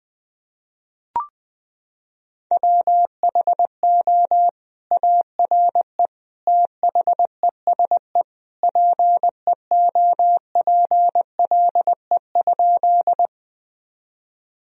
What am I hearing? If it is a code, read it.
Morse code, heard as a keyed tone, about 20 wpm, 700 Hz. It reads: WHO ARE THESE PEOPLE?
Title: Sentences from Top 1000 Words - Review Part 1 - 20wpm